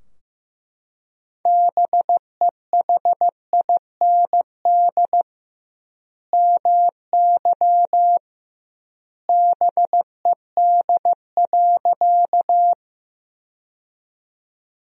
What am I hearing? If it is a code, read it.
Morse code, heard as a keyed tone, about 15 wpm, 700 Hz. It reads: BEHIND MY BED.
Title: 2x Word-Spacing - Sentences from Top 500 Words - 15wpm